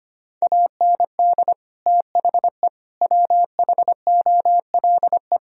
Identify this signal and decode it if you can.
Morse code, heard as a keyed tone, about 25 wpm, 700 Hz. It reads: AND THE WHOLE
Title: Sentences from Top 1000 Words - Review Part 3 - 25wpm